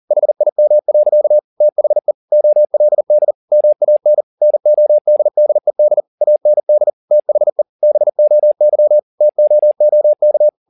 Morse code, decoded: HIM. THE OLD MAN NODDED AND THE BOY TOOK

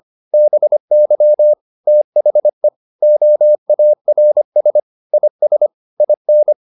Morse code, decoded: BY THE OARS IS IN

